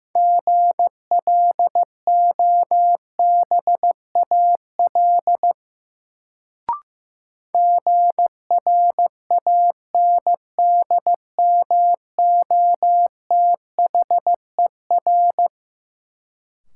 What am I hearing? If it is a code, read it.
Morse code, heard as a keyed tone, about 15 wpm, 700 Hz. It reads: GLOBAL GRANDMOTHER
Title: Sentences from Top 1200 Words - New Word Drill Alt - 15wpm